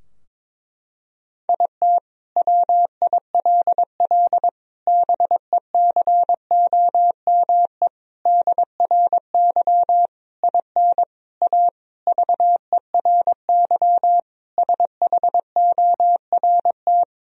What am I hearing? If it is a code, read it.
Morse code, heard as a keyed tone, about 22 wpm, 700 Hz. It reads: IT WILL BECOME DRY IN A VERY SHORT
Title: Sentences from Top 1000 Words - Review Part 3 - 22wpm